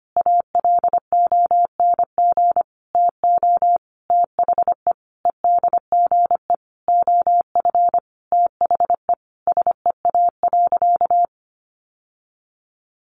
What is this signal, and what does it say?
Morse code, heard as a keyed tone, about 25 wpm, 700 Hz. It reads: ALONG TO THE EDGE OF THE SEA.
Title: Sentences from Top 1300 Words 25wpm